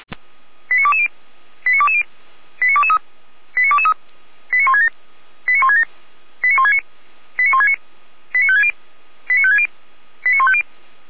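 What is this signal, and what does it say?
Five-tone selective call (ZVEI), heard as quick runs of five tones, tones of about 70 ms. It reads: 89220 89220 89223 89223 89167 89167 89169 89169 89560 89560 89150
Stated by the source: ZVEI five-tone selective call received by SDR, feuerwehr-orlamuende.org (fire brigade Orlamünde) recording.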